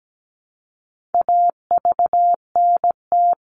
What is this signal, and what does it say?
Morse code, heard as a keyed tone, about 17 wpm, 700 Hz. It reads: AVNT
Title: All Cut Numbers - 4-Digit 17wpm